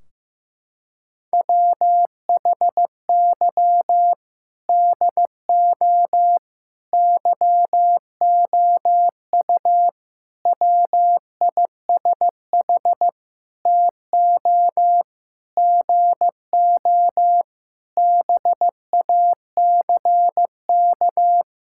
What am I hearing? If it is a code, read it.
Morse code, heard as a keyed tone, about 15 wpm, 700 Hz. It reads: WHY DO YOU WISH TO GO BACK
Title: Sentences from Top 1600 Words 15wpm